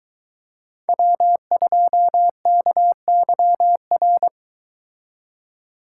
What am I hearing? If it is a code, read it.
Morse code, heard as a keyed tone, about 22 wpm, 700 Hz. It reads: W2KYR